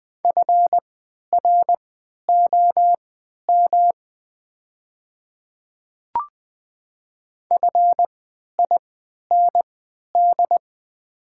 Morse code, heard as a keyed tone, about 20 wpm, spaced as for 11 wpm, 700 Hz. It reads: FROM FIND
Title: 3x Character Spacing - Top 100 Words - 20wpm